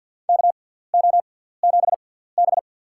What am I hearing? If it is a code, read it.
Morse code, heard as a keyed tone, about 52 wpm, spaced as for 17 wpm, 700 Hz. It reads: KO7B
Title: USA Call Signs - 6x Character Spacing - 50wpm